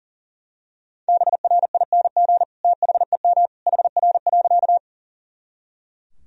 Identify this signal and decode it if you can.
Morse code, heard as a keyed tone, about 40 wpm, 700 Hz. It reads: BRING THEM HR.
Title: Sentences from Top 500 Words and Abbreviations 40wpm